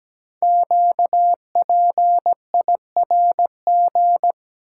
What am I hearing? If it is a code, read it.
Morse code, heard as a keyed tone, about 17 wpm, 700 Hz. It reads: QPIRG
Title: Code Groups - 5 Letters - 17wpm